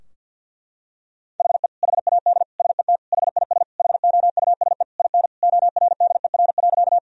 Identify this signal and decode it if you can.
Morse code, heard as a keyed tone, about 50 wpm, 700 Hz. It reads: HE HAD SET HIS HOUSE IN ORDER.
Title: Sentences from Top 1000 Words - Review Part 2 - 50wpm